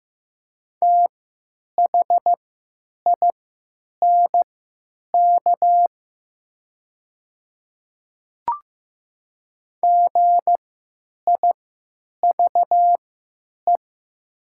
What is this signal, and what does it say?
Morse code, heard as a keyed tone, about 15 wpm, spaced as for 9 wpm, 700 Hz. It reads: THINK GIVE
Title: 3x Character Spacing - Top 500 Words - 15wpm